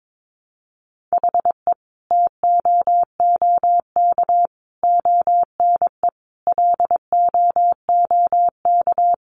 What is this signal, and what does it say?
Morse code, heard as a keyed tone, about 22 wpm, 700 Hz. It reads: HE TOOK ONE LOOK
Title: Sentences from Top 600 Words 22wpm